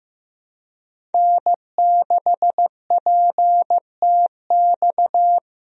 Morse code, heard as a keyed tone, about 15 wpm, 700 Hz. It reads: N6PTX
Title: USA Call Signs - 5 Characters - Repeated 3x - 15wpm